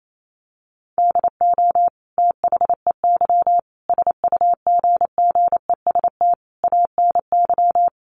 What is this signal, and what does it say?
Morse code, heard as a keyed tone, about 28 wpm, 700 Hz. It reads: DO THEY SUGGEST ANY
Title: Sentences from Top 1300 Words 28wpm